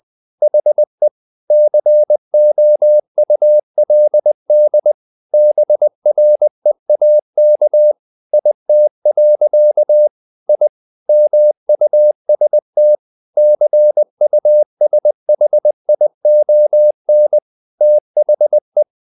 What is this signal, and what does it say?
Morse code, heard as a keyed tone, about 20 wpm, 600 Hz. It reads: HE COULD BREAK IT. I MUST CUSHION THE
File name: the old man and the sea part 2 20wpm